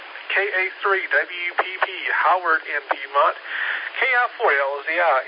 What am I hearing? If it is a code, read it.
Emergency Alert System (SAME) tones, heard as harsh buzzing bursts, with speech. It is heard from 1.0 to 4.3 s.